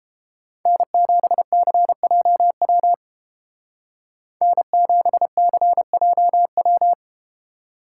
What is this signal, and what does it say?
Morse code, heard as a keyed tone, about 32 wpm, 700 Hz. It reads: N7CJW N7CJW